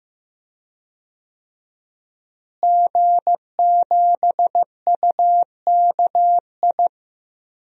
Morse code, heard as a keyed tone, about 15 wpm, 700 Hz. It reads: G7UKI
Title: DXCC-100 Call Signs - 5 Characters - Repeated 3x - 15wpm